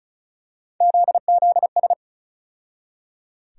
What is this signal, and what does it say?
Morse code, heard as a keyed tone, about 36 wpm, 700 Hz. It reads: ZZS
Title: USA Call Sign Postfixes - 3 Letters 35wpm